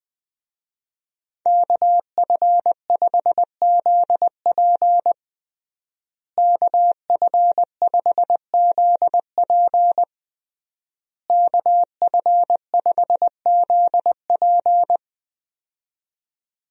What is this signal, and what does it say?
Morse code, heard as a keyed tone, about 20 wpm, 700 Hz. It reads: KF5ZP KF5ZP KF5ZP
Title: USA Call Signs - 5 Characters - Repeated 3x - 20wpm